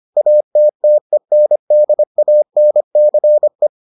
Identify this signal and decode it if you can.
Morse code, heard as a keyed tone, about 25 wpm, 600 Hz. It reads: ATTENDANCE